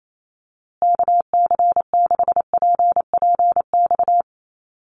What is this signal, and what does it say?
Morse code, heard as a keyed tone, about 28 wpm, 700 Hz. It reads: KC6PPX
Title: USA Call Signs Repeated 3x 28wpm